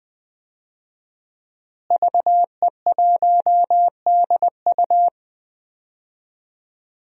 Morse code, heard as a keyed tone, about 20 wpm, 700 Hz. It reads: VE1DU
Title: DXCC-100 Call Signs - 5 Characters - Repeated 2x - 20wpm